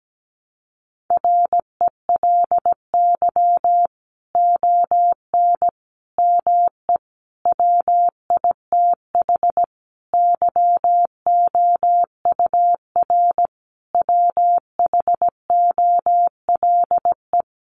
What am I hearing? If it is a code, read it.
Morse code, heard as a keyed tone, about 17 wpm, 700 Hz. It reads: RELY ON ME WITH YOUR WHOLE
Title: Sentences from Top 1200 Words 17wpm